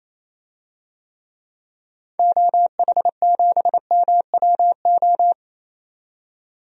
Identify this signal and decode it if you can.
Morse code, heard as a keyed tone, about 28 wpm, 700 Hz. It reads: OH7MWO
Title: DXCC-100 Call Signs - 6 Characters - 28wpm